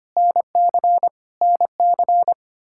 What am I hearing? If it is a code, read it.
Morse code, heard as a keyed tone, about 25 wpm, 700 Hz. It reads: NC NC
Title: Practice QSO - Exchange 1 - Beginning - 25wpm